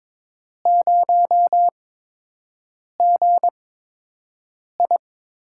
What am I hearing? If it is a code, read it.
Morse code, heard as a keyed tone, about 22 wpm, spaced as for 6 wpm, 700 Hz. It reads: 0GI
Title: USA Call Signs - 6 Characters - 8x Character Spacing - 22wpm